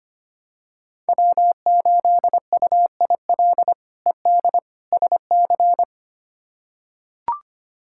Morse code, heard as a keyed tone, about 25 wpm, 700 Hz. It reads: W8UIL ED SC
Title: K1USN SST QSO Practice - 25wpm